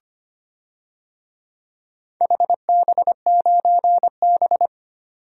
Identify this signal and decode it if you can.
Morse code, heard as a keyed tone, about 25 wpm, 700 Hz. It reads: HB9B
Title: DXCC-100 Call Signs - 4 Characters - 25wpm